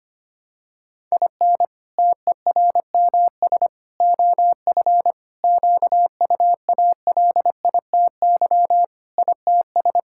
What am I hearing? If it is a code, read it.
Morse code, heard as a keyed tone, about 25 wpm, 700 Hz. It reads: IN TERMS OF QUALITY ITS
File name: practice / 25wpm / Sentences with 3-Word Phrases - 25wpm